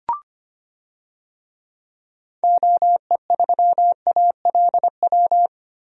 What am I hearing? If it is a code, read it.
Morse code, heard as a keyed tone, about 25 wpm, 700 Hz. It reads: OE3ALW
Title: DXCC-100 Call Signs - 6 Characters - 25wpm